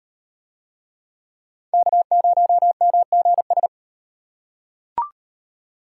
Morse code, heard as a keyed tone, about 38 wpm, 700 Hz. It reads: K0MGS